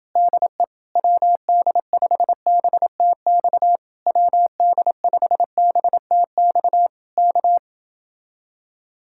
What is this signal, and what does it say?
Morse code, heard as a keyed tone, about 29 wpm, 700 Hz. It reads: DE WD5BTX WD5BTX K